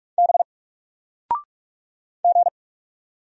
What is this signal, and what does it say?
Morse code, heard as a keyed tone, about 44 wpm, 700 Hz. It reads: B G
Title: Single Letter-Number 45wpm